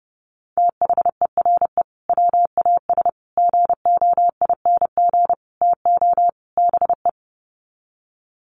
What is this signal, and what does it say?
Morse code, heard as a keyed tone, about 30 wpm, 700 Hz. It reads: THERE WAS GOING TO BE
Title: Sets of 5 Words Encore2 30wpm